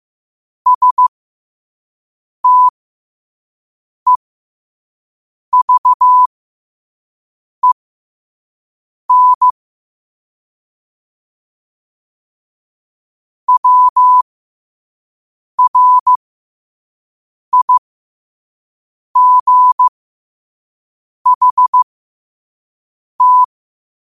Morse code, heard as a keyed tone, about 15 wpm, spaced as for 5 wpm, 1000 Hz. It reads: STEVEN WRIGHT